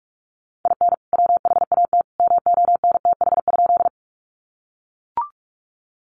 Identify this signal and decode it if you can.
Morse code, heard as a keyed tone, about 44 wpm, 700 Hz. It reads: IN WHAT MONTH?
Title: Sentences from Top 800 Words 45wpm